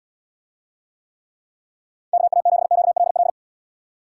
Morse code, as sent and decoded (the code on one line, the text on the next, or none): -.- .- --... --.- .-.. .-..
KA7QLL